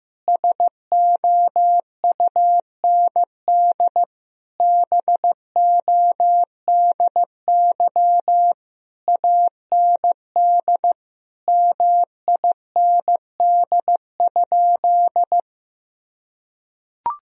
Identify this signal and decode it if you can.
Morse code, heard as a keyed tone, about 15 wpm, 700 Hz. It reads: SOUND BODY AND MIND?
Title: Sentences from Top 1000 Words - Review Part 2 - 15wpm